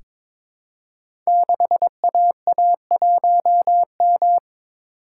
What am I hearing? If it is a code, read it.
Morse code, heard as a keyed tone, about 22 wpm, 700 Hz. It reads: 6AA1M
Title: Code Groups - 5 Letter:Numbers - 22wpm